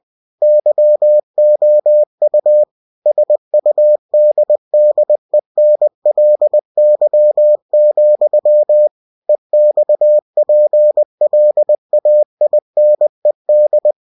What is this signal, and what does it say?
Morse code, heard as a keyed tone, about 20 wpm, 600 Hz. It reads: YOU SUDDENLY, EXPLAINED
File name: winnie-20wpm